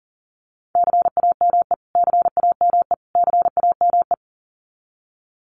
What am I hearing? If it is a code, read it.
Morse code, heard as a keyed tone, about 40 wpm, 700 Hz. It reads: CAME CAME CAME
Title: Top 500 Words Repeated 40wpm